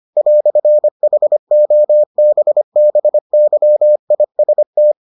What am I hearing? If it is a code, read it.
Morse code, heard as a keyed tone, about 25 wpm, 600 Hz. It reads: "HOBBYIST